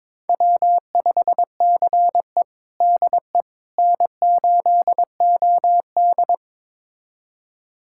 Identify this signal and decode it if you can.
Morse code, heard as a keyed tone, about 22 wpm, 700 Hz. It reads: W5CE DE N8OD
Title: USA Callsign DE Callsign - 4 Characters - 22wpm